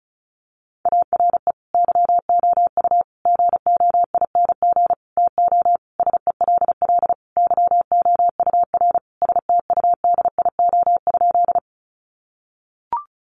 Morse code, heard as a keyed tone, about 34 wpm, 700 Hz. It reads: ARE YOU GOING TO SELL YOUR STUDIO?
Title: Sentences from Top 1200 Words 35wpm